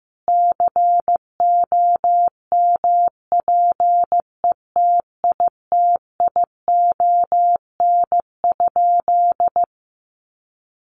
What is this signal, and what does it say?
Morse code, heard as a keyed tone, about 15 wpm, 700 Hz. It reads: COMPETITION?
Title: Sentences from Top 1000 Words 15wpm